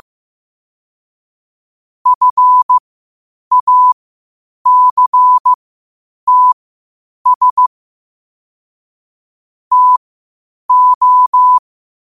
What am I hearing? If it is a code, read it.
Morse code, heard as a keyed tone, about 15 wpm, spaced as for 8 wpm, 1000 Hz. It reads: FACTS TO
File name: qotdfiles / 8wpm